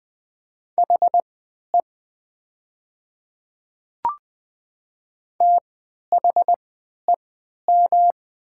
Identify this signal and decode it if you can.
Morse code, heard as a keyed tone, about 20 wpm, spaced as for 11 wpm, 700 Hz. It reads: HE THEM
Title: 3x Character Spacing - Top 100 Words - 20wpm